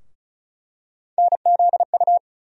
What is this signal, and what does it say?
Morse code, heard as a keyed tone, about 34 wpm, 700 Hz. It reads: NZU
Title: USA Call Sign Postfixes - 3 Letters 35wpm